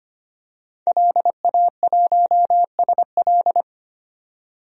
Morse code, heard as a keyed tone, about 25 wpm, 700 Hz. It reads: LA1SL
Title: DXCC-100 Call Signs - 5 Characters - 25wpm